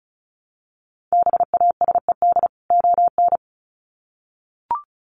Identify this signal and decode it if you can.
Morse code, heard as a keyed tone, about 34 wpm, 700 Hz. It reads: BASED ON
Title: Sets of 2 Words 35wpm